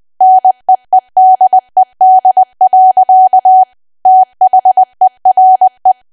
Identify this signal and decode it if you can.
Morse code, heard as a keyed tone, about 20 wpm, 750 Hz. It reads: NEEDED. THERE
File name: cwa / W1AW-Level3 / 20WPM